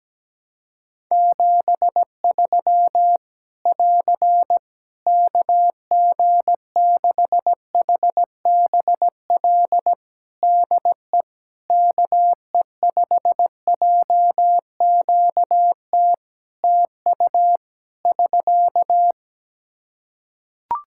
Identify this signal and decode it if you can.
Morse code, heard as a keyed tone, about 17 wpm, 700 Hz. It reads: 73 <AR> KG6HBL DE KE5JQT TU <SK>
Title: Practice QSO - Exchange 4 - Ending - 17wpm